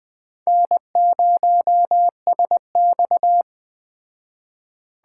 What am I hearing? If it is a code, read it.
Morse code, heard as a keyed tone, about 21 wpm, 700 Hz. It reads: N0SX
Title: Speed Racing - USA 1x2 Call Signs 17wpm